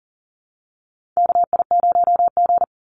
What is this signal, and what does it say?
Morse code, heard as a keyed tone, about 39 wpm, 700 Hz. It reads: KI0G
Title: NA SOTA Chaser Callsigns 40wpm